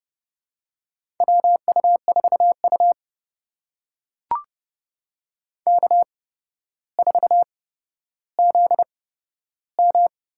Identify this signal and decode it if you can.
Morse code, heard as a keyed tone, about 32 wpm, spaced as for 13 wpm, 700 Hz. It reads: WU4U K4ZM
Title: USA Call Signs - 8x Character Spacing - 30wpm